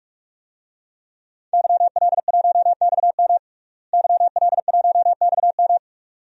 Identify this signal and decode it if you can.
Morse code, heard as a keyed tone, about 42 wpm, 700 Hz. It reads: YL1XM YL1XM
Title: DXCC-100 Call Signs - 5 Characters - Repeated 2x - 45wpm